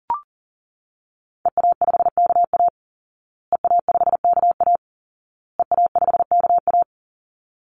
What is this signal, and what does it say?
Morse code, heard as a keyed tone, about 40 wpm, 700 Hz. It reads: EA5KA EA5KA EA5KA
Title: Call Signs Repeated 2x - 40wpm